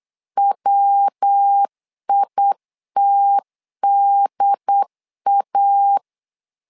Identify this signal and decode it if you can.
Morse code, heard as a keyed tone, about 8 wpm, 800 Hz. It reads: WITDA
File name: LEZIONI / 3 MWRO / 3 MW 8wpm 2501